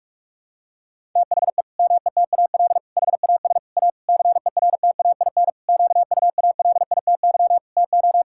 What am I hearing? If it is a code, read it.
Morse code, heard as a keyed tone, about 46 wpm, 700 Hz. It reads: THE METAL HAS A CERTAIN QUALITY TO
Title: Sentences from Top 1000 Words - Review Part 6 - 45wpm